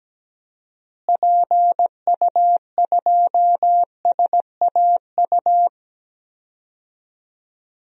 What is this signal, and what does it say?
Morse code, heard as a keyed tone, about 17 wpm, 700 Hz. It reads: PU2SAU
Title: DXCC-100 Call Signs - 6 Characters - Repeated 2x - 17wpm